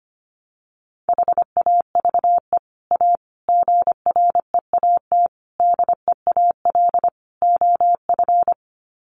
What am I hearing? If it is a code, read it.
Morse code, heard as a keyed tone, about 25 wpm, 700 Hz. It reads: HAVE A GREAT DEAL OF
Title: Sets of 5 Words Encore2 25wpm